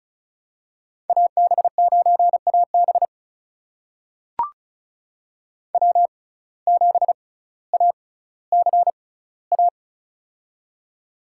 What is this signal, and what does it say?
Morse code, heard as a keyed tone, about 34 wpm, spaced as for 18 wpm, 700 Hz. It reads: AB9AB W7ACA
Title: USA Call Signs - 5 Characters - 6x Character Spacing - 35wpm